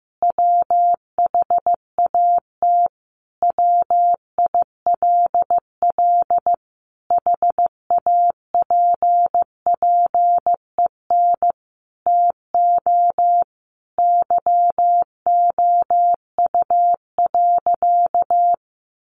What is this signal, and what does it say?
Morse code, heard as a keyed tone, about 15 wpm, 700 Hz. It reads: WHAT WILL HAPPEN TO YOU.
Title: Sentences from Top 500 Words 15wpm